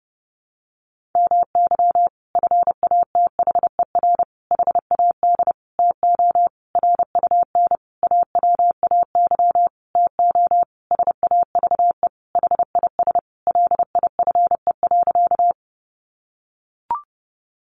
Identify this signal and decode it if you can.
Morse code, heard as a keyed tone, about 30 wpm, 700 Hz. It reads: MY FATHER HAD TO RUN AWAY TO SAVE HIS LIFE.
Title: Sentences from Top 1100 Words 30wpm